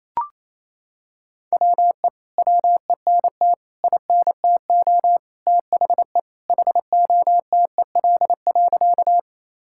Morse code, heard as a keyed tone, about 28 wpm, 700 Hz. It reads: WE WENT INTO THE HOTEL.
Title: Sentences from Top 1600 Words 28wpm